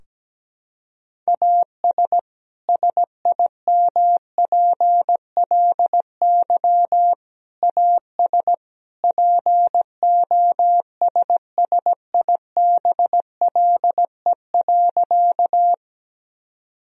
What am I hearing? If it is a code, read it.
Morse code, heard as a keyed tone, about 17 wpm, 700 Hz. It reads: AS SIMPLY AS POSSIBLE.
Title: Sentences from Top 1000 Words - Review Part 4 - 17wpm